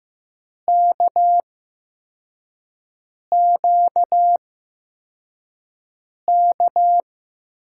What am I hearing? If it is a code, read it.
Morse code, heard as a keyed tone, about 15 wpm, spaced as for 4 wpm, 700 Hz. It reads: KQK
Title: USA Call Signs - 6 Characters - 8x Character Spacing - 15wpm